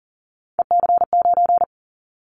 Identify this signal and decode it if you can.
Morse code, heard as a keyed tone, about 42 wpm, 690 Hz. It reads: EC9
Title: Callsign Prefixes for Top 100 DXCC - 40wpm